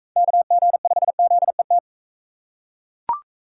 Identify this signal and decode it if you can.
Morse code, heard as a keyed tone, about 39 wpm, 700 Hz. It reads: KG5ZET